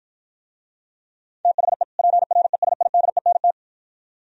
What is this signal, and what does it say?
Morse code, heard as a keyed tone, about 51 wpm, 700 Hz. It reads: THE PRESIDENT